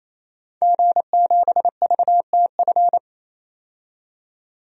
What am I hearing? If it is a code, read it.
Morse code, heard as a keyed tone, about 28 wpm, 700 Hz. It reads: G7VTF
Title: DXCC-100 Call Signs - 5 Characters - 28wpm